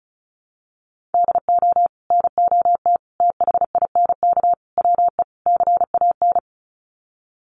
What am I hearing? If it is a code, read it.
Morse code, heard as a keyed tone, about 34 wpm, 700 Hz. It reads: DO NOT THINK WE CAN
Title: Sets of 5 Words Encore2 35wpm